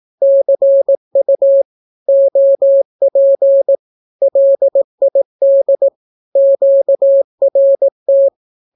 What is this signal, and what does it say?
Morse code, heard as a keyed tone, about 18 wpm, 550 Hz. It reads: CU OP LID QRT
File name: naukapodstaw / skroty / skroty